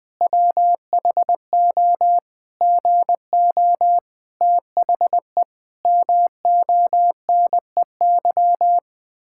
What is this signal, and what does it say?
Morse code, heard as a keyed tone, about 20 wpm, 700 Hz. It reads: WHO GO THE MONEY
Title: Pangram Challenge v2 - 20wpm